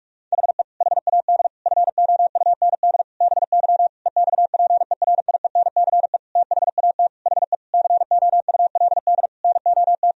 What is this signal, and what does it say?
Morse code, heard as a keyed tone, about 46 wpm, 700 Hz. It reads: HE HAD FOUND BY EXPERIENCE THAT HE COULD NOT
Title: Sentences from Top 1000 Words - Review Part 3 - 45wpm